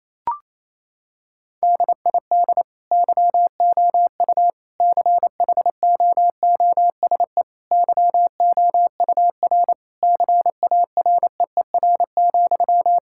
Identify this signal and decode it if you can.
Morse code, heard as a keyed tone, about 28 wpm, 700 Hz. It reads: DID YOU CHOOSE YOUR CAREER,
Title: Sentences from Top 1000 Words - Review Part 6 - 28wpm